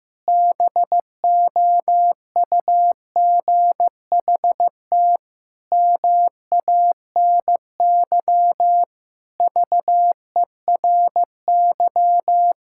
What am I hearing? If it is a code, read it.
Morse code, heard as a keyed tone, about 16 wpm, 700 Hz. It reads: BOUGHT MANY VERY
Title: Pangram Challenge v2 - 15wpm